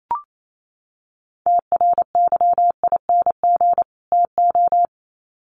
Morse code, heard as a keyed tone, about 28 wpm, 700 Hz. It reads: TRYING TO
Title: Sets of 2 Words 28wpm